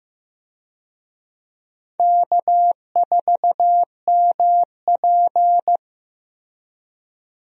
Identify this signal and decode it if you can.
Morse code, heard as a keyed tone, about 15 wpm, 700 Hz. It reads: K4MP
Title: USA Call Signs - 4 Characters - Repeated 3x - 15wpm